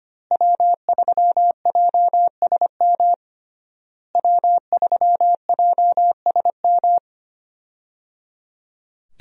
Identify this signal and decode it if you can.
Morse code, heard as a keyed tone, about 25 wpm, 700 Hz. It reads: W3JSM W3JSM
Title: USA Call Signs - 5 Characters - Repeated 3x - 25wpm